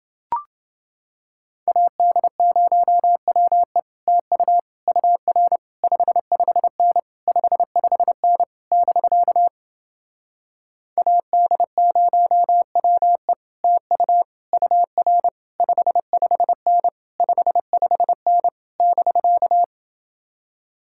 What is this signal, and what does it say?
Morse code, heard as a keyed tone, about 30 wpm, 700 Hz. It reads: AD0WE TU UR 55N 55N <BK> AD0WE TU UR 55N 55N <BK>